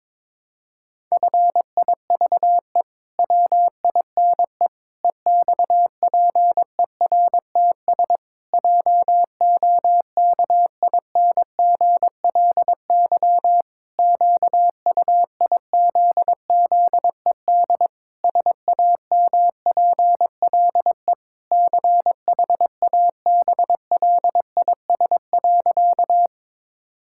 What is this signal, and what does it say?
Morse code, heard as a keyed tone, about 21 wpm, 700 Hz. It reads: FIVE WINE EXPERTS JOKINGLY QUIZZED SAMPLE CHABLIS.
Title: Pangram Challenge v2 - 22wpm